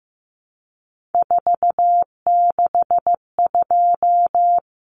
Morse code, heard as a keyed tone, about 15 wpm, 700 Hz. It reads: 462
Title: Common Cut Numbers - 3-Digit 15wpm